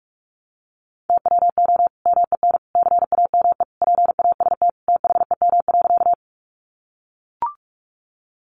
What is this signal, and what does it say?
Morse code, heard as a keyed tone, about 44 wpm, 700 Hz. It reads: TWO MEN CAME PAST THEM.
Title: Sentences from Top 600 Words 45wpm